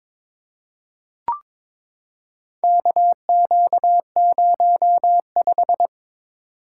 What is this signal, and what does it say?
Morse code, heard as a keyed tone, about 22 wpm, 700 Hz. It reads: KQ05
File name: practice / 22wpm / Four Letter-Number Combinations - 22wpm